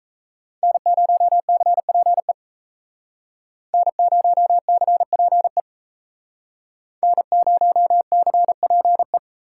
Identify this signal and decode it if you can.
Morse code, heard as a keyed tone, about 36 wpm, 700 Hz. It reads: N0CPE N0CPE N0CPE